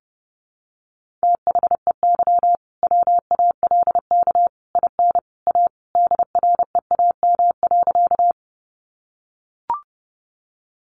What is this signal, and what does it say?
Morse code, heard as a keyed tone, about 30 wpm, 700 Hz. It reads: THEY WALK IN A DREAM.
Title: Sentences from Top 1500 Words 30wpm